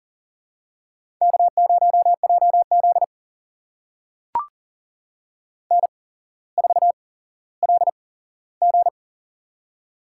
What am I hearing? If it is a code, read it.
Morse code, heard as a keyed tone, about 40 wpm, spaced as for 17 wpm, 700 Hz. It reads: K0JZ N4LG
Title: USA Call Signs - 4 Characters - 8x Character Spacing - 40wpm